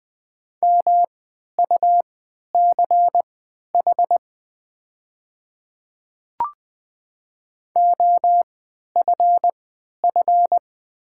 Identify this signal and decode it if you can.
Morse code, heard as a keyed tone, about 20 wpm, spaced as for 11 wpm, 690 Hz. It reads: MUCH OFF
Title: 3x Character Spacing - Top 300 Words - 20wpm